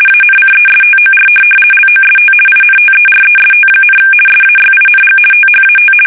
RTTY, about 50 baud. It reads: ANCESTOR LOST INTEREST, SAYING THAT
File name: fsk850-50B / teletype-corp